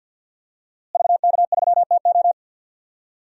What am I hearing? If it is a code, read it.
Morse code, heard as a keyed tone, about 51 wpm, 700 Hz. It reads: VK3TO
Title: DXCC-100 Call Signs - 5 Characters - Repeated 3x - 50wpm